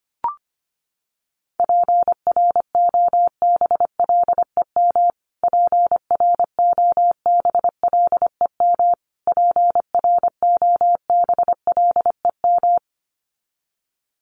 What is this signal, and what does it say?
Morse code, heard as a keyed tone, about 25 wpm, 700 Hz. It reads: PROBLEM PROBLEM PROBLEM
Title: Top 500 Words Repeated 25wpm